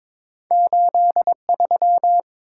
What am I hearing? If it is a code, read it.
Morse code, heard as a keyed tone, about 22 wpm, 700 Hz. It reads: 83
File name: practice / 22wpm / Two Letter-Number Combinations - 22wpm